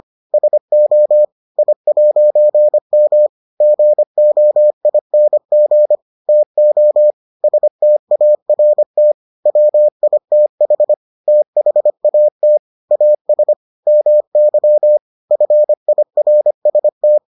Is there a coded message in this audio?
Yes: Morse code from 0.3 to 17.2 s, heard as a keyed tone, about 25 wpm, 600 Hz. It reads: SO I'M GOING TO START WITH THAT AS MY FIRST